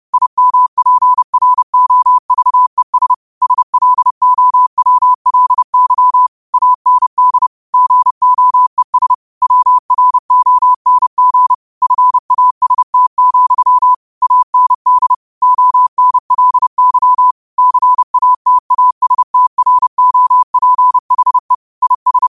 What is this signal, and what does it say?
Morse code, heard as a keyed tone, about 30 wpm, 1000 Hz. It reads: IMPROVES SLOWLY AND GOES WRONG FAST, AND ONLY CATASTROPHE IS
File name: qotdfiles / 30wpm